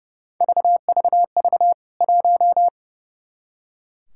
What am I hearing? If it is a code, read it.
Morse code, heard as a keyed tone, about 30 wpm, 700 Hz. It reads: VVV 1
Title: K1USN SST QSO Practice 30wpm